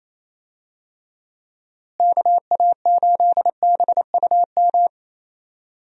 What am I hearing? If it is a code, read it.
Morse code, heard as a keyed tone, about 28 wpm, 700 Hz. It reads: KA8BUM
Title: USA Call Signs - 6 Characters - Repeated 3x - 28wpm